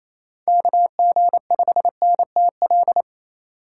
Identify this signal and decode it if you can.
Morse code, heard as a keyed tone, about 27 wpm, 700 Hz. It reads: KG5NTL
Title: USA Call Signs - 2x Character Spacing - 28wpm